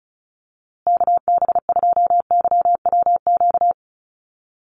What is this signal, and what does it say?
Morse code, heard as a keyed tone, about 34 wpm, 700 Hz. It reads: KB2YWQ
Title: USA Call Signs Repeated 2x 35wpm